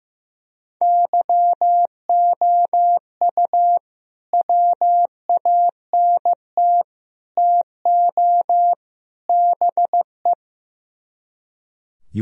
Morse code, heard as a keyed tone, about 15 wpm, 700 Hz. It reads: YOU WANT TO BE
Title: Speed-Racing - Sets of 4 Words 15wpm